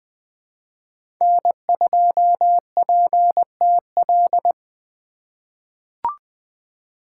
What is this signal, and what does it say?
Morse code, heard as a keyed tone, about 20 wpm, 700 Hz. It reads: N2PTL
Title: USA Call Signs - 5 Characters - 3x Character Spacing - 20wpm